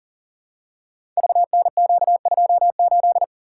Morse code, heard as a keyed tone, about 42 wpm, 690 Hz. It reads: VNQ28
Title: Code Groups - 5 Letter:Numbers - 40wpm